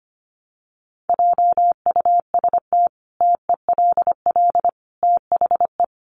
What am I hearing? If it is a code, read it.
Morse code, heard as a keyed tone, about 25 wpm, 700 Hz. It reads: JUST TELL THE
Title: Sentences from Top 800 Words 25wpm